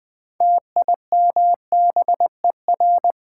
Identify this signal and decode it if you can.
Morse code, heard as a keyed tone, about 20 wpm, 700 Hz. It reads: TIMBER
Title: Build A Word - 2 Letters At A Time - 6-Letter Words - 20wpm